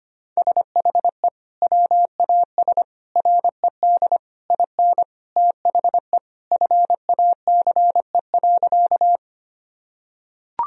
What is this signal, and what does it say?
Morse code, heard as a keyed tone, about 25 wpm, 700 Hz. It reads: SHE WAS RED IN THE FACE.
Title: Sentences from Top 1000 Words - Review Part 2 - 25wpm